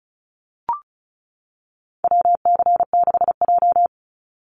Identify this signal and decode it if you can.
Morse code, heard as a keyed tone, about 34 wpm, 700 Hz. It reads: WC6J
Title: NA SOTA Activator Callsigns 35wpm